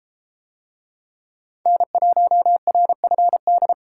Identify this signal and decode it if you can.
Morse code, heard as a keyed tone, about 32 wpm, 700 Hz. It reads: N1RFD